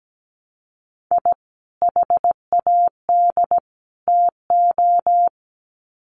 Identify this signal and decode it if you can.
Morse code, heard as a keyed tone, about 17 wpm, 700 Hz. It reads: I HAD TO
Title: Sets of 3 Words 17wpm